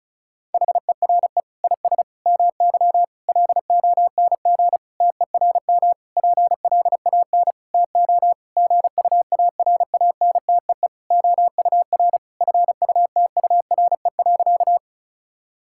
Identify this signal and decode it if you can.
Morse code, heard as a keyed tone, about 36 wpm, 700 Hz. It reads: HERE IS MY LONG TERM PLAN TO GUARANTEE OUR FUTURE.
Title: Sentences from Top 1600 Words 35wpm